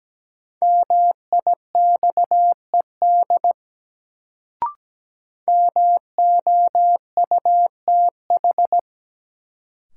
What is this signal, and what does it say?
Morse code, heard as a keyed tone, about 17 wpm, 700 Hz. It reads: MIXED MOUTH
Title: Sentences from Top 1500 Words - New Word Drill - 17wpm